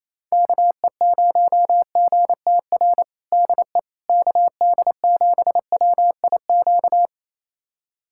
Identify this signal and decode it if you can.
Morse code, heard as a keyed tone, about 28 wpm, 700 Hz. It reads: KE0GTR DE KD7WIQ
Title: USA Callsign DE Callsign - 6 Characters - 28wpm